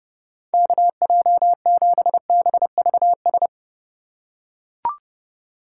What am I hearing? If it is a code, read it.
Morse code, heard as a keyed tone, about 30 wpm, 700 Hz. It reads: KJ7BVS
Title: USA Call Signs - 6 Characters - 3x Character Spacing - 30wpm